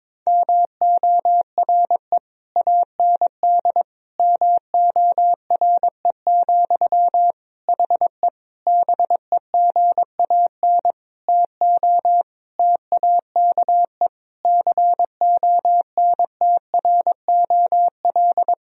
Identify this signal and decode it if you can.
Morse code, heard as a keyed tone, about 22 wpm, 700 Hz. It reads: MORE AND MORE, HE BEGAN TO TAKE CONTROL
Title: Sentences from Top 1000 Words - Review Part 3 - 22wpm